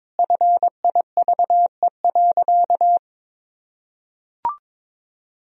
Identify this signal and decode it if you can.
Morse code, heard as a keyed tone, about 22 wpm, 700 Hz. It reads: FIVE.
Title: Sentences from Top 400 Words and Abbreviations 22wpm